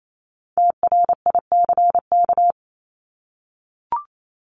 Morse code, heard as a keyed tone, about 28 wpm, 700 Hz. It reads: TRICK
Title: Top 1251-1500 Words 28wpm